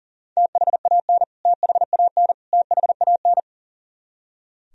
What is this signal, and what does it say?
Morse code, heard as a keyed tone, about 40 wpm, 700 Hz. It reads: THAN THAN THAN
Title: Top 100 Words Repeated 40wpm